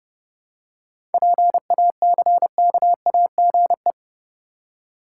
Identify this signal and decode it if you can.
Morse code, heard as a keyed tone, about 30 wpm, 700 Hz. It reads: PACKAGE
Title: Seven Letter Words - 30wpm